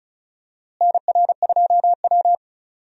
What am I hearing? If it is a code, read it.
Morse code, heard as a keyed tone, about 34 wpm, 700 Hz. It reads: NR2W
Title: USA Call Signs - 4 Characters - Repeated 3x - 35wpm